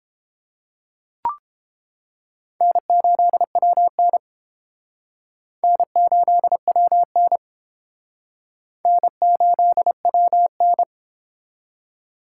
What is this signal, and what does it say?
Morse code, heard as a keyed tone, about 31 wpm, 700 Hz. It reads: N8WN N8WN N8WN